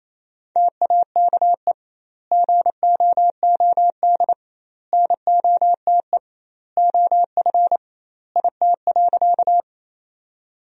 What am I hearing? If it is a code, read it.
Morse code, heard as a keyed tone, about 28 wpm, 700 Hz. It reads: TAKE GOOD NOTE OF IT.
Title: 2x Word-Spacing - Sentences from Top 500 Words - 28wpm